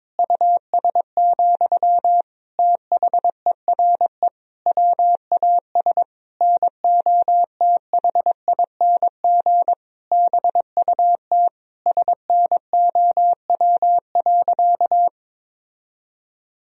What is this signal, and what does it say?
Morse code, heard as a keyed tone, about 22 wpm, 700 Hz. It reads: US, THERE WAS NOTHING BUT SNOW.
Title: Sentences from Top 1000 Words - Review Part 3 - 22wpm